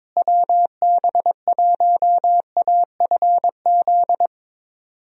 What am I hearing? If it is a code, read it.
Morse code, heard as a keyed tone, about 22 wpm, 700 Hz. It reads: WB1AFZ
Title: USA Call Signs - 6 Characters - Repeated 3x - 22wpm